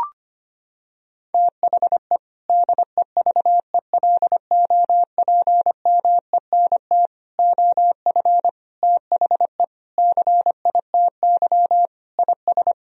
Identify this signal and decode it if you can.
Morse code, heard as a keyed tone, about 25 wpm, 700 Hz. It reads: THE DEVELOPMENT OF THE CITY IS
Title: Sentences from Top 1000 Words - Review Part 4 - 25wpm